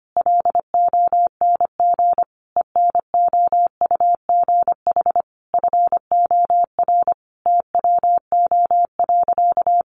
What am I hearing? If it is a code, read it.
Morse code, heard as a keyed tone, about 25 wpm, 700 Hz. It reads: LONG ENOUGH FOR TWO.
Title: Sentences from Top 1400 Words 25wpm